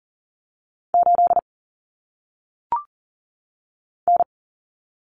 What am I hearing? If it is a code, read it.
Morse code, heard as a keyed tone, about 40 wpm, 700 Hz. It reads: 8 N
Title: Common Cut Numbers - 1-Digit 40wpm